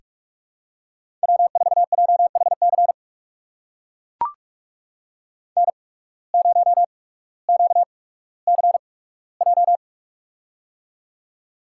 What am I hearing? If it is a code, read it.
Morse code, heard as a keyed tone, about 42 wpm, spaced as for 19 wpm, 700 Hz. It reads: W4JHC N0QCJ
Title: USA Call Signs - 5 Characters - 8x Character Spacing - 45wpm